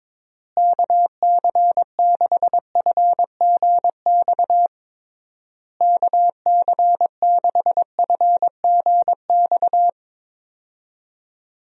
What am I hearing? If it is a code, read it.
Morse code, heard as a keyed tone, about 22 wpm, 700 Hz. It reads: KC6FGX KC6FGX
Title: USA Call Signs - 6 Characters - Repeated 3x - 22wpm